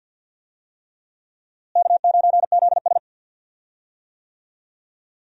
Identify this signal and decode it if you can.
Morse code, heard as a keyed tone, about 46 wpm, 700 Hz. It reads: K9ZS
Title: USA Call Signs - 4 Characters - Repeated 3x - 50wpm